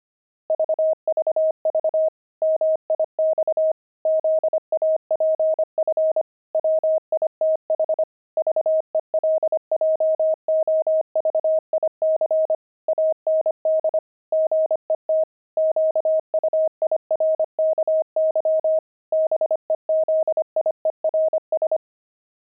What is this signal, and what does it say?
Morse code, heard as a keyed tone, about 25 wpm, 630 Hz. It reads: VVV MIX ZAPF WITH VELJOVIC AND GET QUIRKY BEZIERS
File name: pangram-fs-25-cs-25-zaph